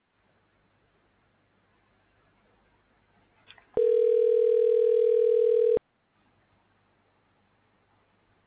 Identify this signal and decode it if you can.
Ringback tone (North America), 440 + 480 Hz.